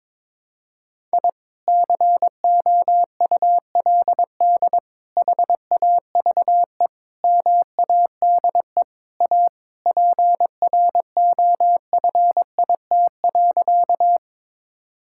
Morse code, heard as a keyed tone, about 21 wpm, 700 Hz. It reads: I COULD HAVE MADE A PROFIT.
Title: Sentences from Top 1000 Words - Review Part 5 - 22wpm